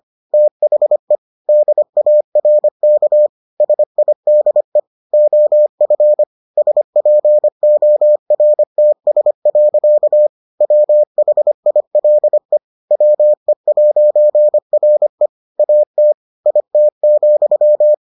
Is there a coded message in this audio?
Yes: Morse code from 0.3 to 17.9 s, heard as a keyed tone, about 25 wpm, 600 Hz. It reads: THE DARK SIDE OF SPORTS. WHILE WE'RE AT IT,